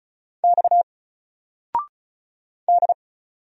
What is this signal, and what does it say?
Morse code, heard as a keyed tone, about 36 wpm, 700 Hz. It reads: X D
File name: Single-Letter-Number-35wpm